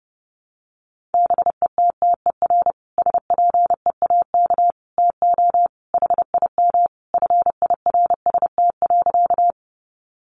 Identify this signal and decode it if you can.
Morse code, heard as a keyed tone, about 30 wpm, 700 Hz. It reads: BETTER SPEAK TO HIM FIRST.
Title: Sentences from Top 1100 Words 30wpm